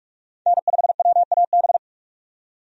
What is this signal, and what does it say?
Morse code, heard as a keyed tone, about 46 wpm, 700 Hz. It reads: N5WAB
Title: USA Call Signs - 5 Characters - 45wpm